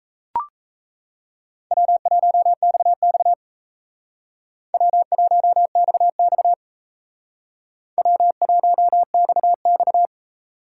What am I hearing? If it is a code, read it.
Morse code, heard as a keyed tone, about 34 wpm, 700 Hz. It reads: W1XX W1XX W1XX
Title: Speed Racing - USA Call Signs 28wpm